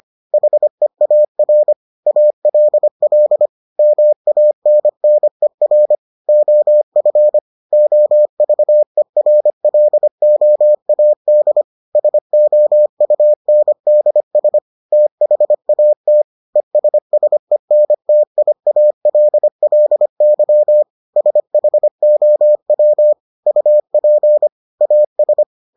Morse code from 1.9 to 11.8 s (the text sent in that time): ALL MANNER OF OVERLOAD